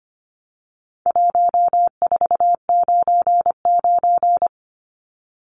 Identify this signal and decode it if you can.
Morse code, heard as a keyed tone, about 25 wpm, 700 Hz. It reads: 1499
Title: Numbers - 4-Digit 25wpm